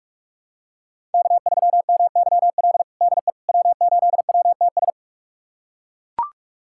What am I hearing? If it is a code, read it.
Morse code, heard as a keyed tone, about 42 wpm, 700 Hz. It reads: K3MYL DE W8WTS
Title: USA Callsign DE Callsign - 5 Characters - 45wpm